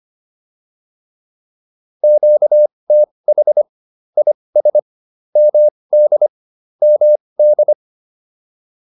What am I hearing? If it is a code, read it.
Morse code, heard as a keyed tone, about 25 wpm, spaced as for 20 wpm, 600 Hz. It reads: QTH IS MD MD
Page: advanced-practice-files